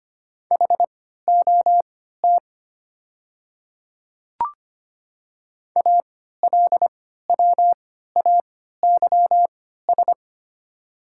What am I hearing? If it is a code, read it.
Morse code, heard as a keyed tone, about 25 wpm, spaced as for 14 wpm, 690 Hz. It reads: HOT ALWAYS
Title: 3x Character Spacing - Top 500 Words - 25wpm